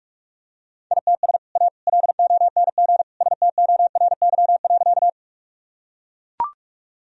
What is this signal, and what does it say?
Morse code, heard as a keyed tone, about 41 wpm, 700 Hz. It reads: ITS A LONG STORY.
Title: Sentences from Top 1000 Words - Review Part 1 - 45wpm